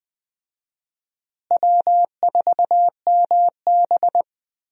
Morse code, heard as a keyed tone, about 21 wpm, 700 Hz. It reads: W4MB
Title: USA Call Signs - 4 Characters - Repeated 2x - 20wpm